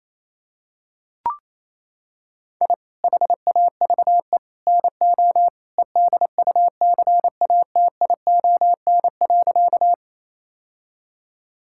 Morse code, heard as a keyed tone, about 27 wpm, 700 Hz. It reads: I HAVE NO EDUCATION.
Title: Sentences from Top 1000 Words - Review Part 6 - 28wpm